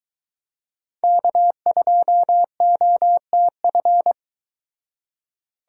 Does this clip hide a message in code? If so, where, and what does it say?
Yes: Morse code from 1.0 to 4.1 s, heard as a keyed tone, about 24 wpm, 700 Hz. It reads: K2OTF